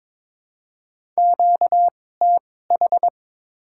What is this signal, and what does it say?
Morse code, heard as a keyed tone, about 22 wpm, spaced as for 16 wpm, 700 Hz. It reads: QTH
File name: Top 20 QSO Elements - 2x Character Spacing - 22wpm